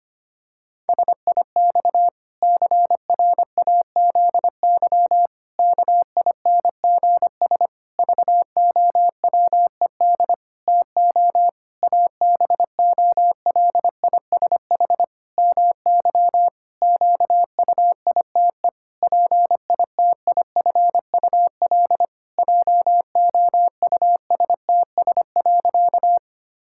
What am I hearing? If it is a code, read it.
Morse code, heard as a keyed tone, about 25 wpm, 700 Hz. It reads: SIX CRAZY KINGS VOWED TO ABOLISH MY QUITE PITIFUL JOUSTS.
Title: Pangram Challenge v2 - 25wpm